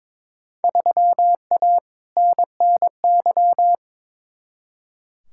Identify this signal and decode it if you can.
Morse code, heard as a keyed tone, about 22 wpm, 700 Hz. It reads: 3A NNY
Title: Field Day Exchanges - 22wpm